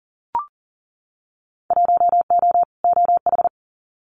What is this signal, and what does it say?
Morse code, heard as a keyed tone, about 39 wpm, 700 Hz. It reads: 1O OH
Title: Winter Field Day Exchanges 40wpm